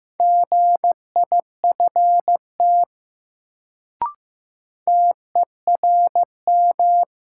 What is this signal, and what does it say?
Morse code, heard as a keyed tone, about 15 wpm, 700 Hz. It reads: GIFT TERM
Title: Sentences from Top 900 Words - New Word Drill - 15wpm